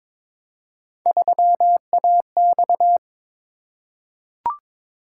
Morse code, heard as a keyed tone, about 22 wpm, 700 Hz. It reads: 3AX
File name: practice / 22wpm / Three Letter-Number Combinations - 22wpm